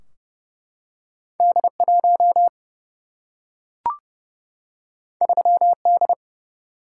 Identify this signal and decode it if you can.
Morse code, heard as a keyed tone, about 31 wpm, 700 Hz. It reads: D 1 3 D
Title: Single-Letter Single-Number 30wpm